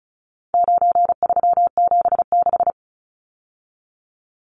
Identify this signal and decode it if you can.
Morse code, heard as a keyed tone, about 34 wpm, 700 Hz. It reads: 9376
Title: Numbers - 4-Digit 35wpm